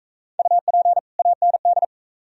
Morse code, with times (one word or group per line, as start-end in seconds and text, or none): 0.4-1.0 UP
1.2-1.9 AND